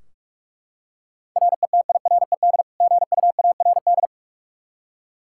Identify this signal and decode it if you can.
Morse code, heard as a keyed tone, about 44 wpm, 700 Hz. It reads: RETIRED GUARD
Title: K1USN SST QSO Practice 45wpm